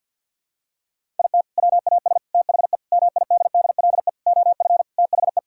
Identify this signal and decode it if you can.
Morse code, heard as a keyed tone, about 48 wpm, 700 Hz. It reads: IT WAS THE MIDDLE OF THE
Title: Sentences from Top 1000 Words - Review Part 6 - 50wpm